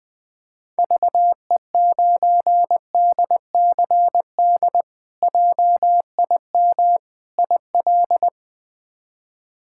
Morse code, heard as a keyed tone, about 20 wpm, 700 Hz. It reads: VE9DCD JIM IL
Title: K1USN SST QSO Practice - 20wpm